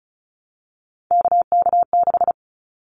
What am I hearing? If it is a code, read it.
Morse code, heard as a keyed tone, about 34 wpm, 700 Hz. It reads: KK6
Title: Callsign Prefixes for Top 25 DXCC - 35wpm